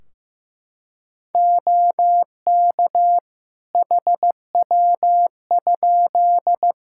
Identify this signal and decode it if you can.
Morse code, heard as a keyed tone, about 15 wpm, 700 Hz. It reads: OK HW?
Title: K1USN SST QSO Practice 15wpm